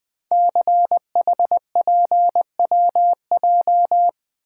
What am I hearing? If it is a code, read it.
Morse code, heard as a keyed tone, about 20 wpm, 700 Hz. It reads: CHPWJ
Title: Code Groups - 5 Letters - 20wpm